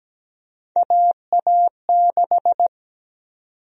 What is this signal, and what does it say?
Morse code, heard as a keyed tone, about 17 wpm, 700 Hz. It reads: AA6
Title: USA Call Sign Prefixes - 17wpm